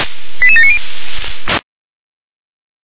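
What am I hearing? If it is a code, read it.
Five-tone selective call (ZVEI), heard as quick runs of five tones, tones of about 70 ms. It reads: 88790
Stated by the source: Five-tone ZVEI selective call received by SDR.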